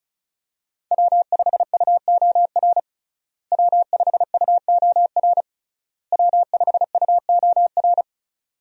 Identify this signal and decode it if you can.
Morse code, heard as a keyed tone, about 36 wpm, 700 Hz. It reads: W5UOR W5UOR W5UOR
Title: USA Call Signs - 5 Characters - Repeated 3x - 35wpm